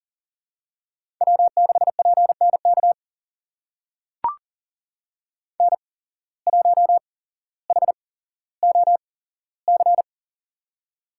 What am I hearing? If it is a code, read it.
Morse code, heard as a keyed tone, about 40 wpm, spaced as for 17 wpm, 700 Hz. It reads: W6PNK N1HOC
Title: USA Call Signs - 5 Characters - 8x Character Spacing - 40wpm